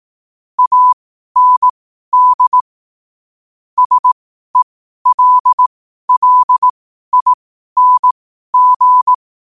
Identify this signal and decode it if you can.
Morse code, heard as a keyed tone, about 18 wpm, spaced as for 12 wpm, 1000 Hz. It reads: AND SELLING
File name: qotdfiles / 12wpm